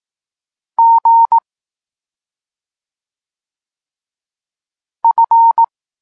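Morse code, heard as a keyed tone, about 18 wpm, 920 Hz. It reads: G F
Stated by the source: Morse code lesson for learning the letters K and X, from the SP9MOA Niepołomicki Klub Krótkofalowców course.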